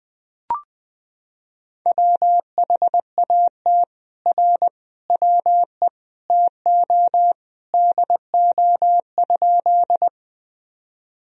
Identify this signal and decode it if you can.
Morse code, heard as a keyed tone, about 20 wpm, 700 Hz. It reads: WHAT R WE TO DO?
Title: Sentences from Top 100 Words and Abbreviations 20wpm